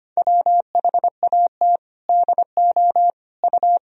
Morse code, heard as a keyed tone, about 25 wpm, 700 Hz. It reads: WHAT DO U
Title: Sentences from Top 100 Words and Abbreviations 25wpm